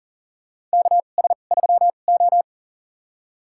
Morse code, heard as a keyed tone, about 40 wpm, spaced as for 29 wpm, 700 Hz. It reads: KS3O
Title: USA Call Signs - 4 Characters - 2x Character Spacing - 40wpm